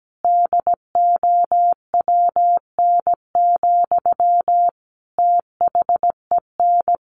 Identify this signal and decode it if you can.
Morse code, heard as a keyed tone, about 17 wpm, 700 Hz. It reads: DOWN, THEN
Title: Sentences from Top 1400 Words 17wpm